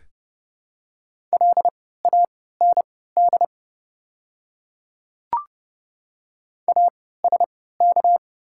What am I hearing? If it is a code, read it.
Morse code, heard as a keyed tone, about 30 wpm, spaced as for 17 wpm, 700 Hz. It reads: LAND ASK
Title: 3x Character Spacing - Top 200 Words - 30wpm